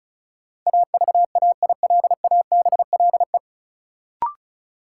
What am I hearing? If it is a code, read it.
Morse code, heard as a keyed tone, about 36 wpm, 700 Hz. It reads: AVAILABLE
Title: Sentences from Top 600 Words - New Word Drill - 35wpm